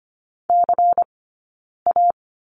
Morse code, heard as a keyed tone, about 25 wpm, 700 Hz. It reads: C A
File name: course / 25:10wpm / 092 Lesson 16d - Letter c - Two Characters 25wpm